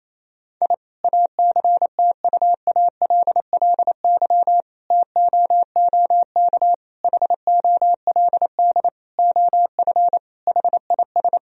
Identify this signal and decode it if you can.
Morse code, heard as a keyed tone, about 28 wpm, 700 Hz. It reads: I ACTUALLY TOOK HOLD OF HIS
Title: Sentences from Top 1000 Words - Review Part 4 - 28wpm